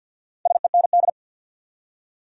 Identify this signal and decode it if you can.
Morse code, heard as a keyed tone, about 52 wpm, 700 Hz. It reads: SEND
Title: Four Letter Words - 50wpm